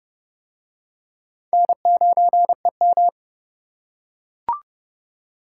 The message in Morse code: N9EM

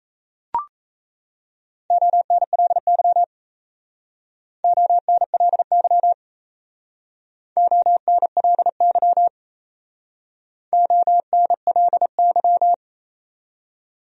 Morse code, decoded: ONLY ONLY ONLY ONLY

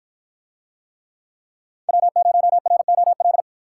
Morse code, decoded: W0ROL

